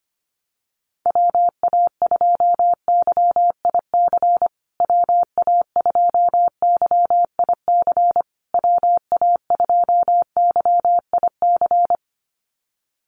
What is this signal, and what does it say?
Morse code, heard as a keyed tone, about 25 wpm, 700 Hz. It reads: WA2YIC WA2YIC WA2YIC
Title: USA Call Signs Repeated 3x 25wpm